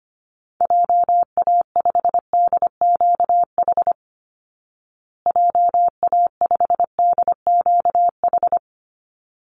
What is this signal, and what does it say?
Morse code, heard as a keyed tone, about 25 wpm, 700 Hz. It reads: JA5DQH JA5DQH
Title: Call Signs Repeated 3x - 25wpm